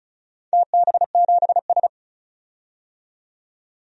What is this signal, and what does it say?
Morse code, heard as a keyed tone, about 36 wpm, 700 Hz. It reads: TB7S
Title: DXCC-100 Call Signs - 4 Characters - 35wpm